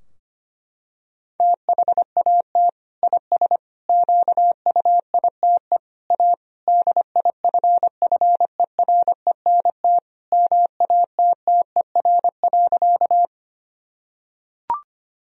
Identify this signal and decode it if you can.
Morse code, heard as a keyed tone, about 25 wpm, 700 Hz. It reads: THAT IS QUITE A DIFFERENT MATTER.
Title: Sentences from Top 1000 Words - Review Part 3 - 25wpm